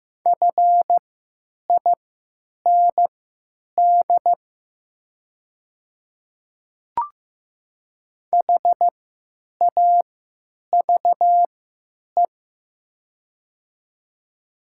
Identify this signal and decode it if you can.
Morse code, heard as a keyed tone, about 15 wpm, spaced as for 9 wpm, 700 Hz. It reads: FIND HAVE
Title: 3x Character Spacing - Top 100 Words - 15wpm